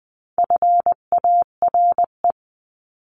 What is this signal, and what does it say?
Morse code, heard as a keyed tone, about 20 wpm, 700 Hz. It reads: FARE
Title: 259 Lesson 42d - Words 18-20wpm